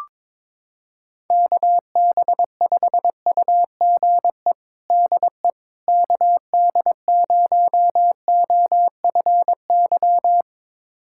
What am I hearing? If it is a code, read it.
Morse code, heard as a keyed tone, about 22 wpm, 700 Hz. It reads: KB5UGE DE KD0OFY
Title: USA Callsign DE Callsign 22wpm